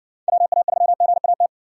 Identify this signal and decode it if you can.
Morse code, heard as a keyed tone, about 60 wpm, 700 Hz. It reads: WA3GAT